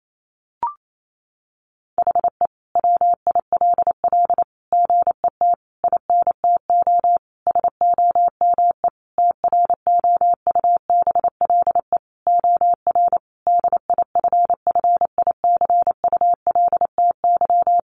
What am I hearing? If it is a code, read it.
Morse code, heard as a keyed tone, about 28 wpm, 700 Hz. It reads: HE WILL GET INTO SOME TROUBLE OR DIFFICULTY
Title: Sentences from Top 1500 Words 28wpm